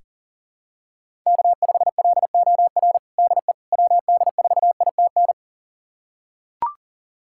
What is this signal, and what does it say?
Morse code, heard as a keyed tone, about 40 wpm, 700 Hz. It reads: K5LOR DE WD4ITN
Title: USA Callsign DE Callsign 40wpm